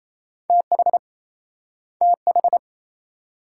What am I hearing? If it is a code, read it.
Morse code, heard as a keyed tone, about 38 wpm, 700 Hz. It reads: TH TH